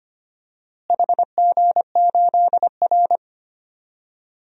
Morse code, heard as a keyed tone, about 25 wpm, 700 Hz. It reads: HG8R
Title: DXCC-100 Call Signs - 4 Characters - Repeated 2x - 25wpm